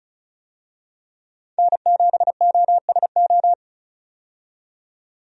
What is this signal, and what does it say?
Morse code, heard as a keyed tone, about 36 wpm, 700 Hz. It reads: N7OSO
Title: USA Call Signs - 5 Characters - 35wpm